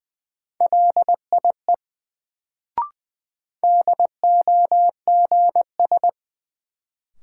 Morse code, heard as a keyed tone, about 20 wpm, 700 Hz. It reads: LIE DOGS
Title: Sentences from Top 1600 Words - New Word Drill - 20wpm